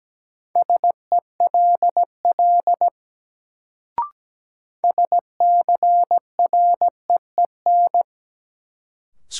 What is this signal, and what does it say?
Morse code, heard as a keyed tone, about 17 wpm, 700 Hz. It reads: SELL SCREEN
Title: Sentences from Top 1000 Words - Review Part 6 - New Word Drill - 17wpm